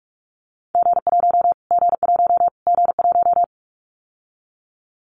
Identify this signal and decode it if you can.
Morse code, heard as a keyed tone, about 44 wpm, 700 Hz. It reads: G1 G1 G1
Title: Callsign Prefixes for Top 25 DXCC - Repeated 3x - 45wpm